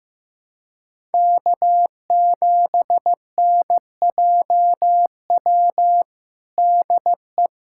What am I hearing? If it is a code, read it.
Morse code, heard as a keyed tone, about 16 wpm, 700 Hz. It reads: K7NJW DE
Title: USA Callsign DE Callsign - 5 Characters - 15wpm